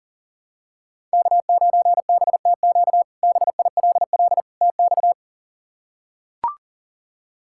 Morse code, heard as a keyed tone, about 37 wpm, 700 Hz. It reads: K9BTQ BILL TX
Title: K1USN SST QSO Practice - 40wpm